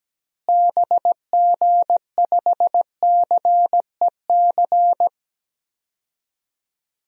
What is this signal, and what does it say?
Morse code, heard as a keyed tone, about 17 wpm, 700 Hz. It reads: BG5CEC
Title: DXCC-100 Call Signs - 6 Characters - Repeated 3x - 17wpm